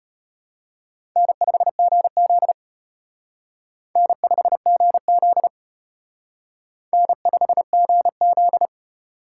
Morse code, heard as a keyed tone, about 32 wpm, 700 Hz. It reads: N5GZ N5GZ N5GZ